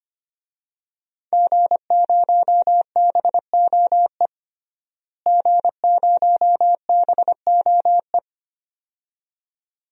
Morse code, heard as a keyed tone, about 25 wpm, 700 Hz. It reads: G0BOE G0BOE
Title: DXCC-100 Call Signs - 5 Characters - Repeated 2x - 25wpm